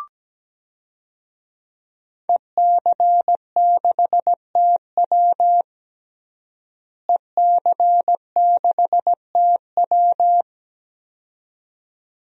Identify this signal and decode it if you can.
Morse code, heard as a keyed tone, about 17 wpm, 700 Hz. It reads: EC6TW EC6TW
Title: DXCC-100 Call Signs - 5 Characters - Repeated 2x - 17wpm